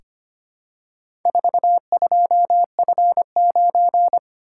Morse code, heard as a keyed tone, about 25 wpm, 700 Hz. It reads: 42F9
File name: practice / 25wpm / Four Letter-Number Combinations - 25wpm